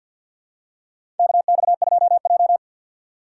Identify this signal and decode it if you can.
Morse code, heard as a keyed tone, about 50 wpm, 700 Hz. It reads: KX2J
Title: USA Call Signs - 2x Character Spacing - 50wpm